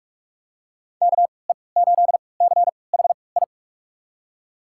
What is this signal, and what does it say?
Morse code, heard as a keyed tone, about 44 wpm, spaced as for 25 wpm, 690 Hz. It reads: KE8CHI
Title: USA Call Signs - 6 Characters - 3x Character Spacing - 45wpm